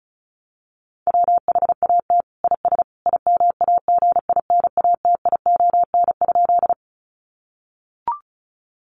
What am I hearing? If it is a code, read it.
Morse code, heard as a keyed tone, about 34 wpm, 700 Hz. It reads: WHAT IS IMAGINATION?
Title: Sentences from Top 1300 Words 35wpm